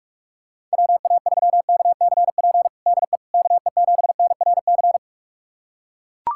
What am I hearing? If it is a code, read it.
Morse code, heard as a keyed tone, about 41 wpm, 700 Hz. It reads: WA3KCP DE KE7NRC
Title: USA Callsign DE Callsign - 6 Characters - 45wpm